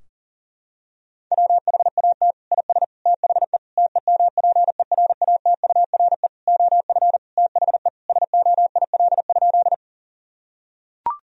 Morse code, heard as a keyed tone, about 40 wpm, 700 Hz. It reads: WHAT IS THE TEMPERATURE OF THE SOIL?
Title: Sentences from Top 1000 Words - Review Part 4 - 40wpm